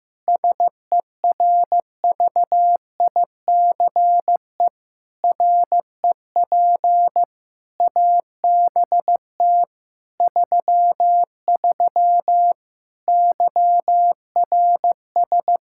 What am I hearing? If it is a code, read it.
Morse code, heard as a keyed tone, about 15 wpm, 700 Hz. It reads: SERVICE REP ABT 33 YRS
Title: Practice QSO - Exchange 3 - More Info - 15wpm